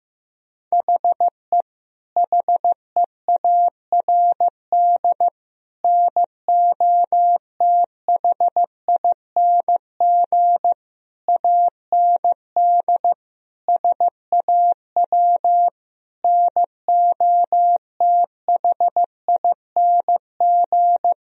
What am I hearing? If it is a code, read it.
Morse code, heard as a keyed tone, about 15 wpm, 700 Hz. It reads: HE HEARD NOTHING AND SAW NOTHING
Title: Sentences from Top 1000 Words - Review Part 3 - 15wpm